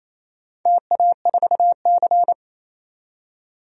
Morse code, heard as a keyed tone, about 28 wpm, 700 Hz. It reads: TA4C
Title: DXCC-100 Call Signs - 4 Characters - 28wpm